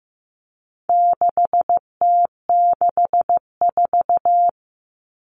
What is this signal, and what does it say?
Morse code, heard as a keyed tone, about 15 wpm, 700 Hz. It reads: 6T64
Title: All Cut Numbers - 4-Digit 15wpm